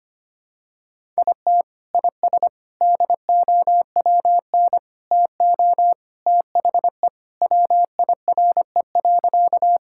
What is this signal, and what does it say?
Morse code, heard as a keyed tone, about 25 wpm, 700 Hz. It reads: IT IS DOWN TO THE WIRE.
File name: practice / 25wpm / Sentences with 3-Word Phrases - 25wpm